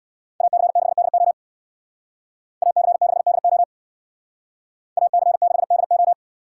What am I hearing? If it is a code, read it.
Morse code, heard as a keyed tone, about 63 wpm, 700 Hz. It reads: AC6DO AC6DO AC6DO